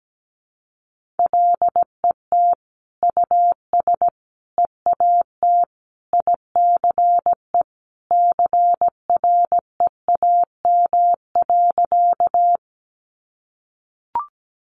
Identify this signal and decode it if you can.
Morse code, heard as a keyed tone, about 17 wpm, 700 Hz. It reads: LET US EAT ICE CREAM.
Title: Sentences from Top 1500 Words 17wpm